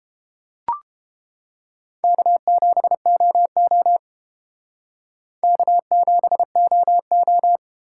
Morse code, heard as a keyed tone, about 31 wpm, 700 Hz. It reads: K7OO K7OO